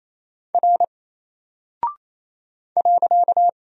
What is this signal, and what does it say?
Morse code, heard as a keyed tone, about 27 wpm, 700 Hz. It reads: R .
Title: Single Character (Letter:Number:Punctuation:Prosign) 28wpm